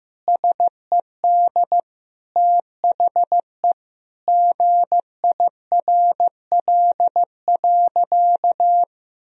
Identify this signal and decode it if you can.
Morse code, heard as a keyed tone, about 15 wpm, 700 Hz. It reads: SED THE GIRL.
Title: Sentences from Top 400 Words and Abbreviations 15wpm